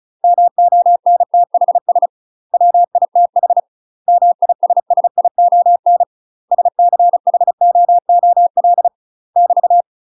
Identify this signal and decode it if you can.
Morse code, heard as a keyed tone, about 34 wpm, 700 Hz. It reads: MONTHS WITH MISSION SCHOOL <BT>
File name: UK MASTER QSOs TRAINING / 35WPM / UK MASTER QSOs 0000